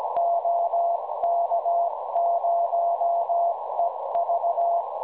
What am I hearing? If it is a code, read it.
Morse code, heard as a keyed tone, about 17 wpm, 690 Hz, with noise. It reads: OK0EV
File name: WebSDR records / OK0EV websdr recording 1854.0kHz